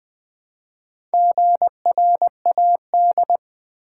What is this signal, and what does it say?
Morse code, heard as a keyed tone, about 20 wpm, 700 Hz. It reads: GRAD
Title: Build A Word - 2 Letters At A Time - 8-Letter Words - 20wpm